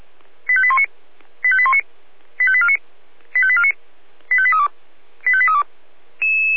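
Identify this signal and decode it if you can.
Five-tone selective call (ZVEI), heard as quick runs of five tones, tones of about 70 ms. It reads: 85819 85819 85849 85849 85832 85832
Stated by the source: Received on an SDR; ZVEI five-tone call 85312.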